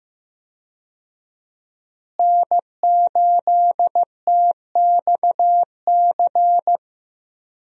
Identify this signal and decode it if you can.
Morse code, heard as a keyed tone, about 15 wpm, 700 Hz. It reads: N8TXC
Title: USA Call Signs - 5 Characters - Repeated 2x - 15wpm